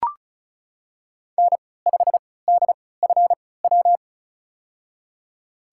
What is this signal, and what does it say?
Morse code, heard as a keyed tone, about 36 wpm, spaced as for 20 wpm, 700 Hz. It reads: N5DFW
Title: USA Call Signs - 3x Character Spacing - 35wpm